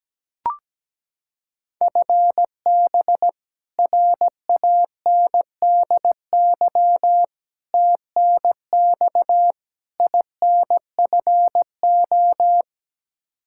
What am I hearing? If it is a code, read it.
Morse code, heard as a keyed tone, about 17 wpm, 700 Hz. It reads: FB RANDY TNX INFO
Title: Practice QSO - Exchange 3 - More Info - 17wpm